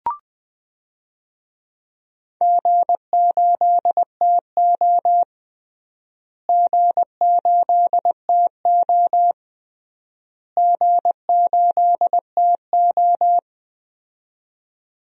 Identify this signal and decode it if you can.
Morse code, heard as a keyed tone, about 20 wpm, 700 Hz. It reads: G8TO G8TO G8TO
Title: DXCC-100 Call Signs - 4 Characters - Repeated 3x - 20wpm